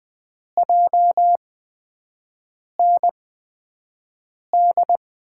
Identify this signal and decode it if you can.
Morse code, heard as a keyed tone, about 20 wpm, spaced as for 5 wpm, 700 Hz. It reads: JND
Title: USA Call Signs - 5 Characters - 8x Character Spacing - 20wpm